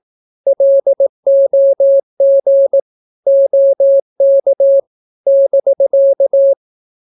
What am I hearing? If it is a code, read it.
Morse code, heard as a keyed tone, about 18 wpm, 550 Hz. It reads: LOG OK <BK>
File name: naukapodstaw / skroty / skroty